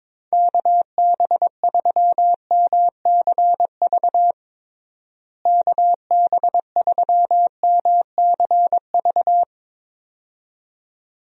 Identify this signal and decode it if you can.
Morse code, heard as a keyed tone, about 22 wpm, 700 Hz. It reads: KB3MCV KB3MCV
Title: USA Call Signs - 6 Characters - Repeated 2x - 22wpm